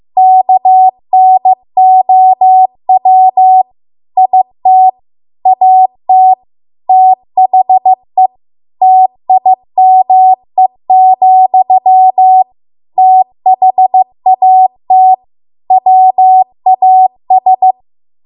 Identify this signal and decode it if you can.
Morse code, heard as a keyed tone, about 15 wpm, 750 Hz. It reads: KNOW IT AT THE TIME, THAT WAS